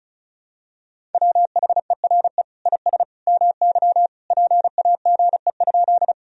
Morse code, heard as a keyed tone, about 34 wpm, 700 Hz. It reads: WHERE IS MY PAGE?
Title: Sentences from Top 1000 Words - Review Part 1 - 35wpm